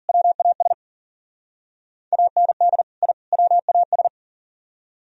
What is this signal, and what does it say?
Morse code, heard as a keyed tone, about 51 wpm, 700 Hz. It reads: WAS AND I WAS